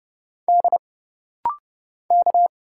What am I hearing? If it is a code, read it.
Morse code, heard as a keyed tone, about 30 wpm, 700 Hz. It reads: D K
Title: Single Letters - Rapid-Fire 30wpm